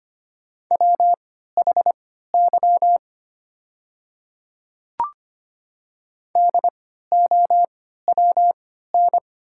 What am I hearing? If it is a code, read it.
Morse code, heard as a keyed tone, about 25 wpm, spaced as for 14 wpm, 690 Hz. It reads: WHY DOWN
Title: 3x Character Spacing - Top 200 Words - 25wpm